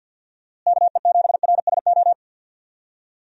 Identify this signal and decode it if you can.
Morse code, heard as a keyed tone, about 53 wpm, 700 Hz. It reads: KE7RSO
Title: USA Call Signs - 6 Characters - Repeated 3x - 50wpm